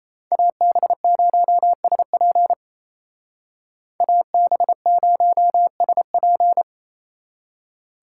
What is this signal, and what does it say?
Morse code, heard as a keyed tone, about 36 wpm, 700 Hz. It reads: AB0SP AB0SP